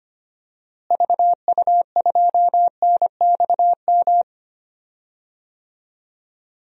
Morse code, heard as a keyed tone, about 25 wpm, 700 Hz. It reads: VU2NXM
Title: DXCC-100 Call Signs - 6 Characters - Repeated 3x - 25wpm